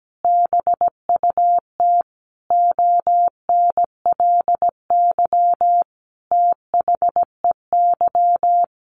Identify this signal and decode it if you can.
Morse code, heard as a keyed tone, about 17 wpm, 700 Hz. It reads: BUT ONLY THEY
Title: Sentences from Top 1500 Words 17wpm